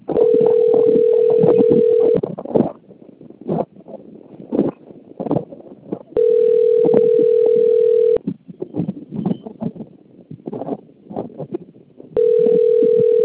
Ringback tone (North America), 440 + 480 Hz.
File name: rg-1600-unknown- - - .30989